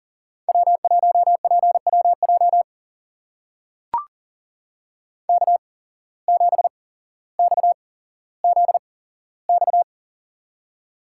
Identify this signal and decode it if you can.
Morse code, heard as a keyed tone, about 40 wpm, spaced as for 17 wpm, 700 Hz. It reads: W1PWJ K7XZX
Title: USA Call Signs - 5 Characters - 8x Character Spacing - 40wpm